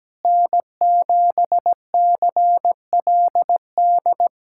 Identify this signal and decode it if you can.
Morse code, heard as a keyed tone, about 17 wpm, 700 Hz. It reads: N7CLD
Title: USA Callsign DE Callsign - 5 Characters - 17wpm